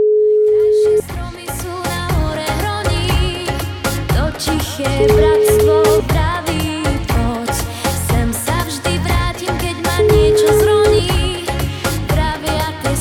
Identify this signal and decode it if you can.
Ringback tone (Europe, 425 Hz), with noise.